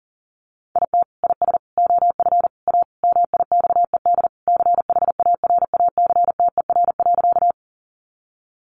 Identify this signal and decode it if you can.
Morse code, heard as a keyed tone, about 40 wpm, 700 Hz. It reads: IT IS OF A MIXED CHARACTER.
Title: Sentences from Top 1500 Words 40wpm